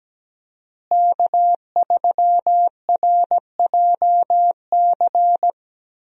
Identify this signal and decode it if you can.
Morse code, heard as a keyed tone, about 17 wpm, 700 Hz. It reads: K3RJC
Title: USA Call Signs - 5 Characters - Repeated 2x - 17wpm